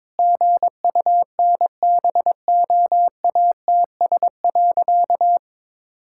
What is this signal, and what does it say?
Morse code, heard as a keyed tone, about 22 wpm, 700 Hz. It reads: GUNBOATS.
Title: Pangram Challenge v2 - 22wpm